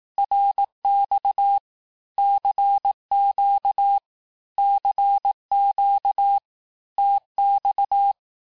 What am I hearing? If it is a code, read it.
Morse code, heard as a keyed tone, about 18 wpm, 800 Hz. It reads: RX CQ CQ TX
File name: cours-du-lundi files / cours-1588-texte-abreviations-vitesse-17mpm-puis-22mpm